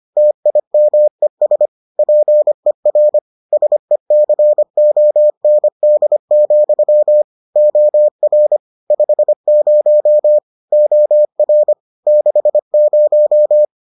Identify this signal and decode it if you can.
Morse code, heard as a keyed tone, about 25 wpm, 600 Hz. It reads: TIMES PER SECOND, OR 50 OR 60